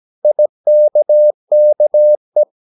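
Morse code, heard as a keyed tone, about 17 wpm, 600 Hz. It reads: IKKE
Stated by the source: Slow (QRS) Morse practice at 17 wpm, from CW med Gnister.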